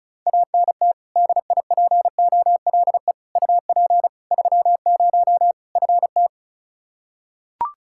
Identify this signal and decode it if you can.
Morse code, heard as a keyed tone, about 36 wpm, 700 Hz. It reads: ANT DIPOLE UP 30 FT
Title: K1USN SST QSO Practice 35wpm